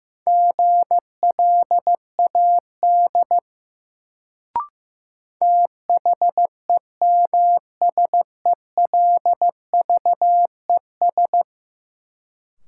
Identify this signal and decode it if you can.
Morse code, heard as a keyed tone, about 15 wpm, 700 Hz. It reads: GLAD THEMSELVES
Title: Sentences from Top 800 Words - New Word Drill - 15wpm